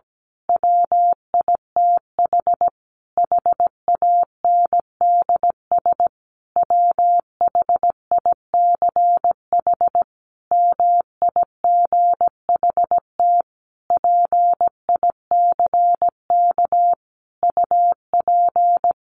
Morse code, heard as a keyed tone, about 17 wpm, 700 Hz. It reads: WITH HANDS WHICH MIGHT PICK UP
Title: A Sound Of Thunder 17wpm